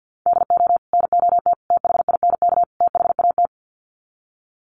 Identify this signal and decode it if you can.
Morse code, heard as a keyed tone, about 50 wpm, 700 Hz. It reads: DO NOT THINK THAT
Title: Sets of 4 Words 50wpm